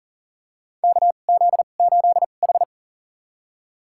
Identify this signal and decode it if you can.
Morse code, heard as a keyed tone, about 40 wpm, spaced as for 29 wpm, 700 Hz. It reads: KZ8H
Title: USA Call Signs - 4 Characters - 2x Character Spacing - 40wpm